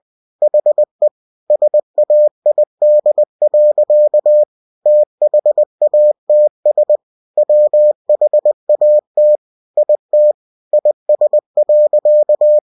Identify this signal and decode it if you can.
Morse code, heard as a keyed tone, about 20 wpm, 600 Hz. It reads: HE SAID. THATS WHAT IT IS.
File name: cw-books / winnie-the-pooh / winnie-20wpm-page-0044